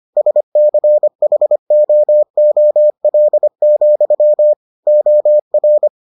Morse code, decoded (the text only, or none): SCHOOL, OR